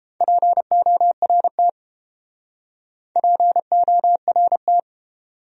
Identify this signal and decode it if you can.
Morse code, heard as a keyed tone, about 28 wpm, 700 Hz. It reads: PORT PORT